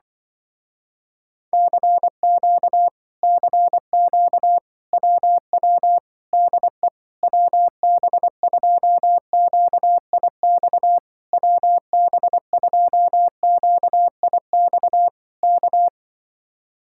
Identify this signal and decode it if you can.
Morse code, heard as a keyed tone, about 24 wpm, 700 Hz. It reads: CQ CQ WW DE WB2QIX WB2QIX K
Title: Speed-Racing Common QSO Phrases Encore - 20wpm